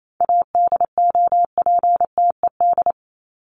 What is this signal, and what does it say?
Morse code, heard as a keyed tone, about 28 wpm, 700 Hz. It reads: ADOPTED
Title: ED Suffix - Top 100 Words with Base Words 28wpm